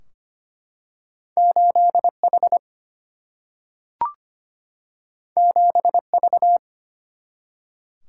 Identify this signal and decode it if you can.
Morse code, heard as a keyed tone, about 25 wpm, 700 Hz. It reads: 8H 7V
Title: Single-Letter Single-Number 25wpm